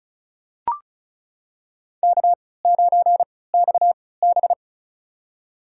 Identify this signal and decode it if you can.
Morse code, heard as a keyed tone, about 34 wpm, spaced as for 20 wpm, 700 Hz. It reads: K9XB
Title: USA Call Signs - 4 Characters - 3x Character Spacing - 35wpm